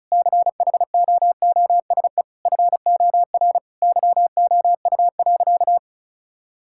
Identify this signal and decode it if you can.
Morse code, heard as a keyed tone, about 36 wpm, 700 Hz. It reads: CHOOSE FOR YOU.
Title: Sentences from Top 1000 Words - Review Part 6 - 35wpm